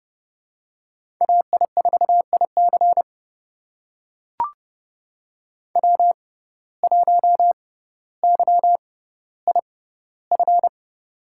Morse code, heard as a keyed tone, about 29 wpm, spaced as for 15 wpm, 700 Hz. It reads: AI4IC W1YIF
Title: USA Call Signs - 5 Characters - 6x Character Spacing - 30wpm